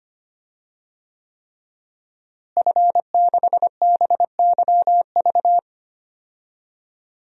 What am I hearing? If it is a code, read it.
Morse code, heard as a keyed tone, about 25 wpm, 700 Hz. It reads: F6BYV
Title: DXCC-100 Call Signs - 5 Characters - Repeated 3x - 25wpm